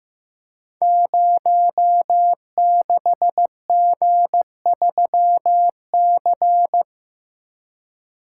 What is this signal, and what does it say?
Morse code, heard as a keyed tone, about 15 wpm, 700 Hz. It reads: 06G3C
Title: Code Groups - 5 Letter:Numbers - 15wpm